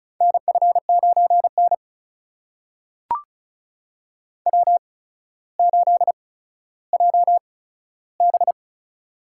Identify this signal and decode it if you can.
Morse code, heard as a keyed tone, about 36 wpm, spaced as for 15 wpm, 700 Hz. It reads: NF9N W8JB
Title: USA Call Signs - 4 Characters - 8x Character Spacing - 35wpm